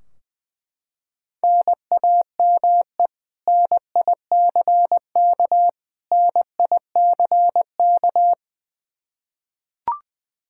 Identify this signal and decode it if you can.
Morse code, heard as a keyed tone, about 20 wpm, 700 Hz. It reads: NAME NICK NICK
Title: Practice QSO - Exchange 1 - Beginning - 20wpm